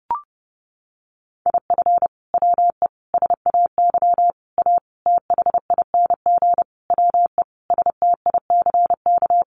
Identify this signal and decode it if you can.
Morse code, heard as a keyed tone, about 30 wpm, 700 Hz. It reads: IF WE SAY A THING WE STICK
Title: Sentences from Top 1200 Words 30wpm